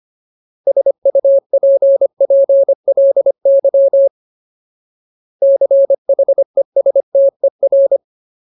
Morse code, heard as a keyed tone, about 25 wpm, 550 Hz. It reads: SUPPLY CHESTER
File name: Ultimate Word List / 25WPM Ultimate